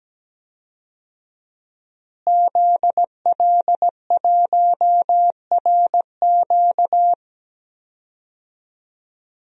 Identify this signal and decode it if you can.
Morse code, heard as a keyed tone, about 17 wpm, 700 Hz. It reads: ZL1RQ
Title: DXCC-100 Call Signs - 5 Characters - Repeated 2x - 17wpm